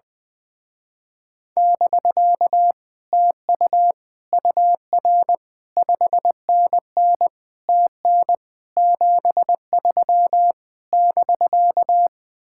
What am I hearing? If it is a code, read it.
Morse code, heard as a keyed tone, about 20 wpm, 700 Hz. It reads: <BK> TU UR 5NN TN 73 <BK>
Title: Speed-Racing Common QSO Phrases Encore - 20wpm